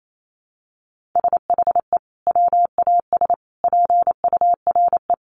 Morse code, heard as a keyed tone, about 28 wpm, 700 Hz. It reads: SHE WAS PURE
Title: Sentences from Top 1500 Words 28wpm